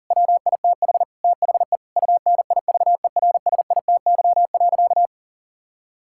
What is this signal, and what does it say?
Morse code, heard as a keyed tone, about 40 wpm, 700 Hz. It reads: WITH THE UNIVERSITY.
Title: Sentences from Top 1000 Words - Review Part 6 - 40wpm